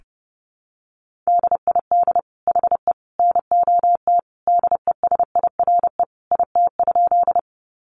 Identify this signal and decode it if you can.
Morse code, heard as a keyed tone, about 30 wpm, 700 Hz. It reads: DID HE NOT DESIRE IT?
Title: Sentences from Top 1100 Words 30wpm